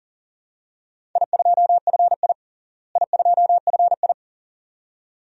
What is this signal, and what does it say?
Morse code, heard as a keyed tone, about 40 wpm, 700 Hz. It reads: I2FI I2FI
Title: DXCC-100 Call Signs - 4 Characters - Repeated 2x - 40wpm